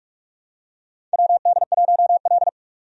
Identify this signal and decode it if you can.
Morse code, heard as a keyed tone, about 46 wpm, 700 Hz. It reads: WD1L
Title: USA Call Signs - 4 Characters - 45wpm